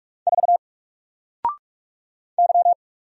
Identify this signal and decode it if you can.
Morse code, heard as a keyed tone, about 44 wpm, 700 Hz. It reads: 4 Y
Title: Single Letter-Number Alt 45wpm